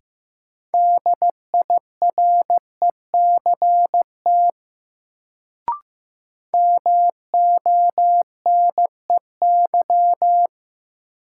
Morse code, heard as a keyed tone, about 15 wpm, 700 Hz. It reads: DIRECT MONEY
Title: Sentences from Top 400 Words - New Word Drill - 15wpm